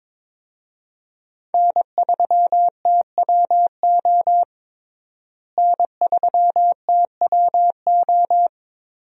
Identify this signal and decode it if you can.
Morse code, heard as a keyed tone, about 21 wpm, 700 Hz. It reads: N3TWO N3TWO
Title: USA Call Signs - 5 Characters - Repeated 2x - 22wpm